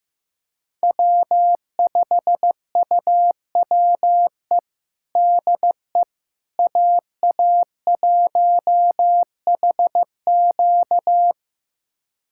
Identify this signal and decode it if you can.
Morse code, heard as a keyed tone, about 16 wpm, 700 Hz. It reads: W5UWE DE AA1HQ
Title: USA Callsign DE Callsign - 5 Characters - 15wpm